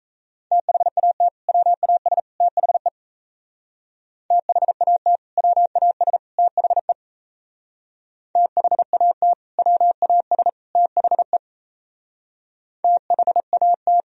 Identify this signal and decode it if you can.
Morse code, heard as a keyed tone, about 36 wpm, 700 Hz. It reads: THAT WAS THE THAT WAS THE THAT WAS THE THAT